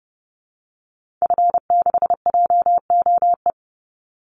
Morse code, heard as a keyed tone, about 30 wpm, 700 Hz. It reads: F6JOE
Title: Call Signs Repeated 3x - 30wpm